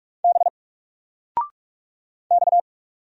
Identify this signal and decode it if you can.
Morse code, heard as a keyed tone, about 44 wpm, 700 Hz. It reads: B X
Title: Single Character (Letter:Number:Punctuation:Prosign) 45wpm